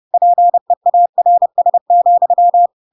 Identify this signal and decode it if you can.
Morse code, heard as a keyed tone, about 30 wpm, 700 Hz. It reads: PEARS,
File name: Pinocchio / Pinocchio ch7 30 30 ik1hge